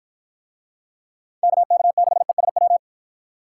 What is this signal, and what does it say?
Morse code, heard as a keyed tone, about 50 wpm, 700 Hz. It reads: KK6ESW